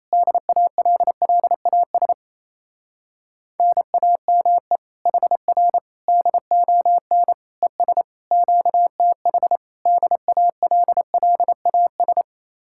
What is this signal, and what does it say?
Morse code, heard as a keyed tone, about 28 wpm, 700 Hz. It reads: DALLAS NAME HR DON ES QTH DALLAS
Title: Speed-Racing Common QSO Phrases Encore - 28wpm